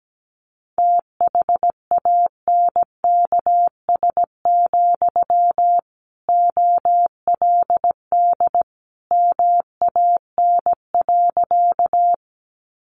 Morse code, heard as a keyed tone, about 17 wpm, 700 Hz. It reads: THANKS, OLD MAN.
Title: Sentences from Top 600 Words 17wpm